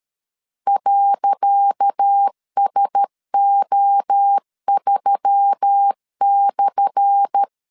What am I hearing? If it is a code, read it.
Morse code, heard as a keyed tone, about 13 wpm, 800 Hz. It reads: .SO3/
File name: LEZIONI / 10 SIGNS / 10C DOT 12wpm 2451